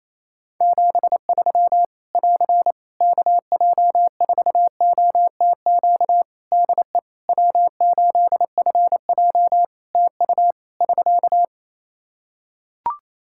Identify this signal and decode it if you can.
Morse code, heard as a keyed tone, about 28 wpm, 700 Hz. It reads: 73 <AR> KJ4OTQ DE W8FJ TU <SK>
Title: Practice QSO - Exchange 4 - Ending - 28wpm